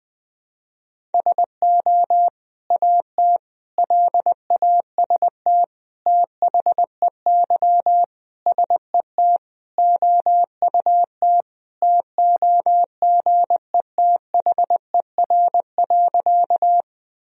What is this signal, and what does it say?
Morse code, heard as a keyed tone, about 21 wpm, 700 Hz. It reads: SO AT LAST THEY SET OUT TOGETHER.
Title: Sentences from Top 1000 Words - Review Part 1 - 20wpm